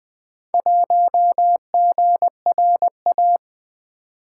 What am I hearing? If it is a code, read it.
Morse code, heard as a keyed tone, about 20 wpm, 700 Hz. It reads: 1GRA
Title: Four Letter-Number Combinations - 20wpm